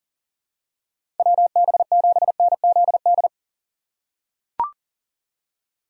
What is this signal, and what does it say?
Morse code, heard as a keyed tone, about 37 wpm, 700 Hz. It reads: WB7NZD
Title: USA Call Signs - 6 Characters - 3x Character Spacing - 40wpm